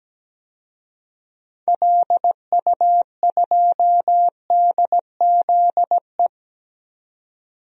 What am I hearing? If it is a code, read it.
Morse code, heard as a keyed tone, about 17 wpm, 700 Hz. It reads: LU2DZE
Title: DXCC-100 Call Signs - 6 Characters - 17wpm